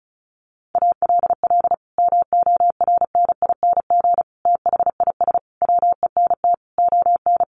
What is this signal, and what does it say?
Morse code, heard as a keyed tone, about 34 wpm, 700 Hz. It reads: ALL MORNING THIS WENT ON
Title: Sentences from Top 1500 Words 35wpm